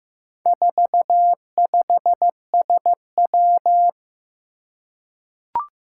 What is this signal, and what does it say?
Morse code, heard as a keyed tone, about 15 wpm, 700 Hz. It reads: 45SW
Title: Four Challenging Letter-Number Combinations 15wpm